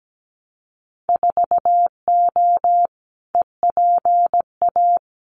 Morse code, heard as a keyed tone, about 17 wpm, 700 Hz. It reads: 4O EPA
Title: Winter Field Day Exchanges 17wpm